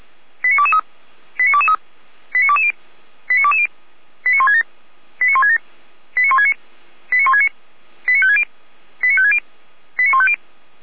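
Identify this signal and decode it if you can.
Five-tone selective call (ZVEI), heard as quick runs of five tones, tones of about 70 ms. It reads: 89223 89223 89220 89220 89167 89167 89169 89169 89560 89560 89150
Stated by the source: ZVEI five-tone selective call 89001 received by SDR, from audio server of Feuerwehr Orlamünde (feuerwehr-orlamuende).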